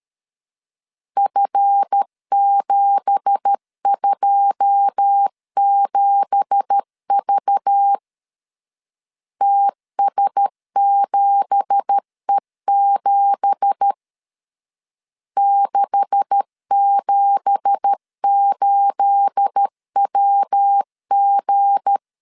Morse code, heard as a keyed tone, about 13 wpm, 800 Hz. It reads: F727V TS7E7 678WG
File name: LEZIONI / 9 2867 12wpm 2322